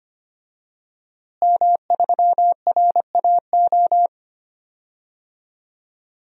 Morse code, heard as a keyed tone, about 25 wpm, 700 Hz. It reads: M3RAO
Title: DXCC-100 Call Signs - 5 Characters - Repeated 3x - 25wpm